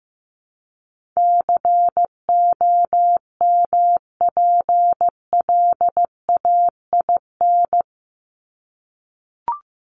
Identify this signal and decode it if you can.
Morse code, heard as a keyed tone, about 15 wpm, 700 Hz. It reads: COMPLAIN
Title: Top 1001-1250 Words 15wpm